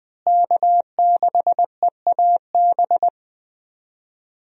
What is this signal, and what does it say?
Morse code, heard as a keyed tone, about 21 wpm, 700 Hz. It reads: K6EAB
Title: USA Call Signs - 5 Characters - Repeated 2x - 20wpm